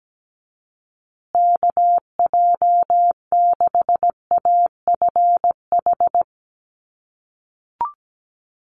Morse code, heard as a keyed tone, about 17 wpm, 700 Hz. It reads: KJ6AFH
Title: USA Call Signs Repeated 3x 17wpm